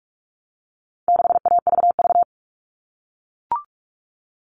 Morse code, heard as a keyed tone, about 44 wpm, 700 Hz. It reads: 6AVV
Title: All Cut Numbers - 4-Digit 45wpm